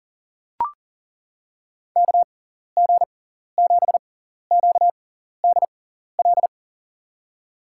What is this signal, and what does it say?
Morse code, heard as a keyed tone, about 38 wpm, spaced as for 14 wpm, 700 Hz. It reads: KG7QDL
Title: USA Call Signs - 6 Characters - 6x Character Spacing - 40wpm